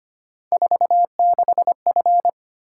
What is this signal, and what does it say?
Morse code, heard as a keyed tone, about 25 wpm, 700 Hz. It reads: 46F
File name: Three Letter-Number Combinations - 25wpm